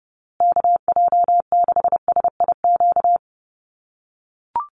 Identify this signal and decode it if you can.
Morse code, heard as a keyed tone, about 30 wpm, 700 Hz. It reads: KJ6SIQ
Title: USA Call Signs Repeated 2x 30wpm